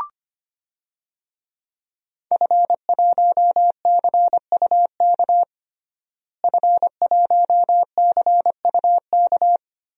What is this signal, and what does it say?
Morse code, heard as a keyed tone, about 25 wpm, 700 Hz. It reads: F1CUK F1CUK
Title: DXCC-100 Call Signs - 5 Characters - Repeated 3x - 25wpm